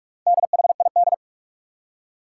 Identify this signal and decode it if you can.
Morse code, heard as a keyed tone, about 46 wpm, 700 Hz. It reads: DHID
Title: Four Challenging Letter-Number Combinations 45wpm